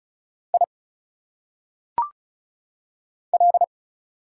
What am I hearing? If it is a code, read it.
Morse code, heard as a keyed tone, about 34 wpm, 700 Hz. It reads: I L
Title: Single Letters - Phonetic Pronunciation 35wpm